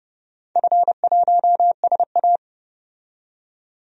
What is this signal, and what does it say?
Morse code, heard as a keyed tone, about 30 wpm, 700 Hz. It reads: F1SA
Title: DXCC-100 Call Signs - 4 Characters - 30wpm